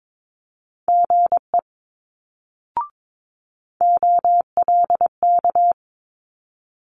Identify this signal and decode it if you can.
Morse code, heard as a keyed tone, about 22 wpm, 700 Hz. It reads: GE OLK
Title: Multi-letter Phonemes Quick Fire 22wpm